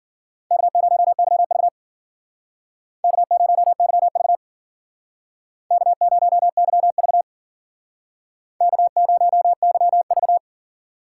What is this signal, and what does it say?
Morse code, heard as a keyed tone, about 51 wpm, 700 Hz. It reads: K0YV K0YV K0YV K0YV